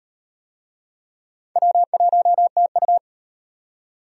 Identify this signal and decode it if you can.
Morse code, heard as a keyed tone, about 38 wpm, 700 Hz. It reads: W1TU